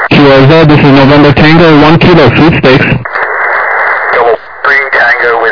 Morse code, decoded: none